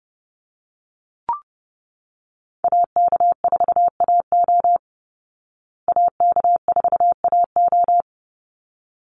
Morse code, heard as a keyed tone, about 29 wpm, 700 Hz. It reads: AK4AO AK4AO
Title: Call Signs Repeated 3x - 30wpm